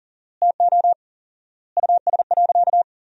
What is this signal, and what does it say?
Morse code, heard as a keyed tone, about 38 wpm, 700 Hz. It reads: TO US.
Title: 4x Word-Spacing - Sentences from Top 500 Words - 40wpm